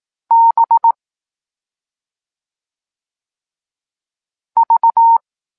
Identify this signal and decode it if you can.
Morse code, heard as a keyed tone, about 18 wpm, 920 Hz. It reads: B V
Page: Tekst 14 – doskonalenie odbioru poznanych liter